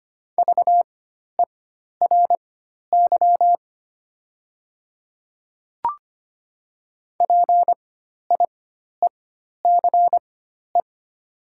Morse code, heard as a keyed tone, about 25 wpm, spaced as for 12 wpm, 690 Hz. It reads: VERY PIECE
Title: 4x Character Spacing - Top 500 Words - 25wpm